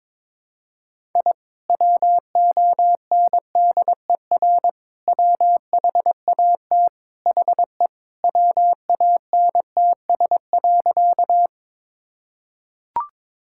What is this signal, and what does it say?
Morse code, heard as a keyed tone, about 22 wpm, 700 Hz. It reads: I WONDER WHAT HE WANTS.
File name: practice / 22wpm / Sentences with 2-Word Phrases - 22wpm